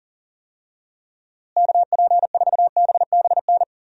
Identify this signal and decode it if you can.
Morse code, heard as a keyed tone, about 40 wpm, 700 Hz. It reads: KP4BBN
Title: DXCC-100 Call Signs - 6 Characters - Repeated 2x - 40wpm